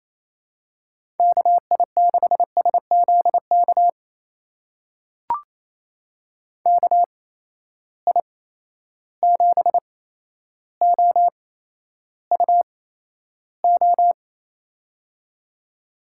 Morse code, heard as a keyed tone, about 28 wpm, spaced as for 12 wpm, 700 Hz. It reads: KI6SZK KI7OUO
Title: USA Call Signs - 6 Characters - 8x Character Spacing - 28wpm